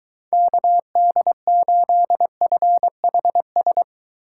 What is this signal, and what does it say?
Morse code, heard as a keyed tone, about 21 wpm, 700 Hz. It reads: KD8FHS